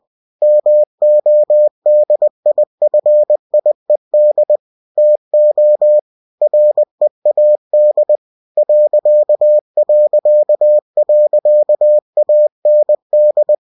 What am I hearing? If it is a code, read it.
Morse code, heard as a keyed tone, about 20 wpm, 600 Hz. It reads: MODIFIED TO READ ...AND
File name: cw-books / winnie-the-pooh / winnie-20wpm-page-0111